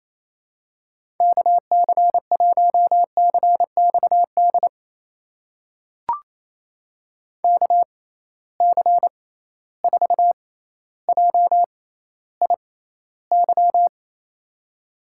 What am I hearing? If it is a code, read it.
Morse code, heard as a keyed tone, about 28 wpm, spaced as for 14 wpm, 700 Hz. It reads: KC1CXD KC4JIY
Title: USA Call Signs - 6 Characters - 6x Character Spacing - 28wpm